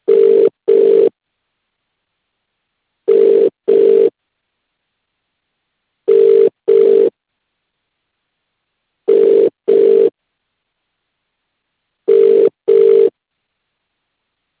Ringback tone (UK), 400 + 450 Hz.